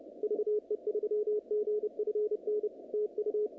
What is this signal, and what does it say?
Morse code, heard as a keyed tone, about 31 wpm, 440 Hz. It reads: VE3GFN TU
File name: contests / naqp-cw-k5tr-op-k5pi- -2200